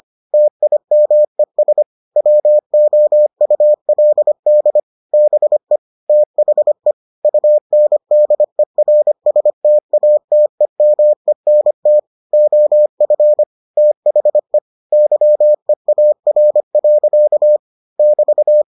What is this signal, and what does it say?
Morse code, heard as a keyed tone, about 25 wpm, 600 Hz. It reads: TIMES WOULD BE THE UNDERSTATEMENT OF THE YEAR. <BT>